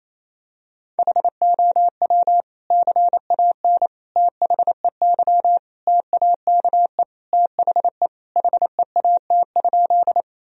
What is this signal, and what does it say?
Morse code, heard as a keyed tone, about 28 wpm, 700 Hz. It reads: HOW CAN THEY TAKE THE HEAT?
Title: Sentences from Top 1000 Words - Review Part 3 - 28wpm